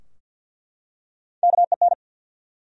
Morse code, heard as a keyed tone, about 50 wpm, 700 Hz. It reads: KEN
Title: USA Common Names - Max 4 Letters - 50wpm